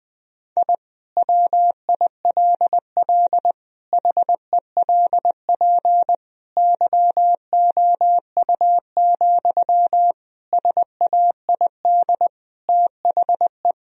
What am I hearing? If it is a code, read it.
Morse code, heard as a keyed tone, about 20 wpm, 700 Hz. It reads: I WILL HELP YOU, SAID THE
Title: Sentences from Top 1000 Words - Review Part 3 - 20wpm